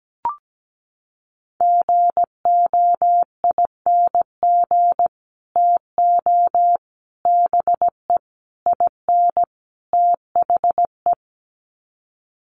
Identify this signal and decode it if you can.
Morse code, heard as a keyed tone, about 17 wpm, 700 Hz. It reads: GOING TO BE IN THE
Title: Sets of 5 Words Encore 17wpm